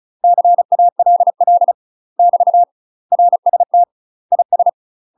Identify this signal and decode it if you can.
Morse code, heard as a keyed tone, about 36 wpm, 700 Hz. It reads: CALL <BT> RST IS
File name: UK MASTER QSOs TRAINING/35WPM